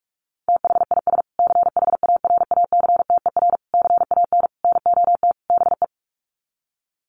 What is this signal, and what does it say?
Morse code, heard as a keyed tone, about 44 wpm, 700 Hz. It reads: THIS CHARACTER CAN NOT BE
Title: Sets of 5 Words Encore2 45wpm